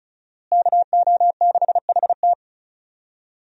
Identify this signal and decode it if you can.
Morse code, heard as a keyed tone, about 34 wpm, 700 Hz. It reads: KO6HT
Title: USA Call Signs - 5 Characters - Repeated 2x - 35wpm